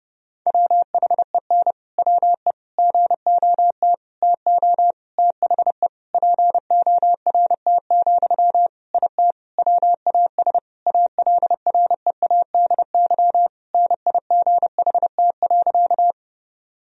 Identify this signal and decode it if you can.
Morse code, heard as a keyed tone, about 30 wpm, 700 Hz. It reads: WHEN WE GOT TO THE PORT, IT WAS ALREADY NIGHT.
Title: Sentences from Top 1000 Words - Review Part 1 - 30wpm